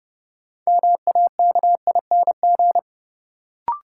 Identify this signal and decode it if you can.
Morse code, heard as a keyed tone, about 30 wpm, 700 Hz. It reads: MAKING
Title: Sentences from Top 1000 Words - Review Part 6 - New Word Drill - 30wpm